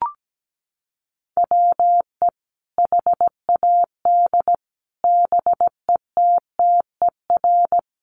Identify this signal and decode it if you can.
Morse code, heard as a keyed tone, about 17 wpm, 700 Hz. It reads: WE HAD BETTER
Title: Sentences from Top 400 Words 17wpm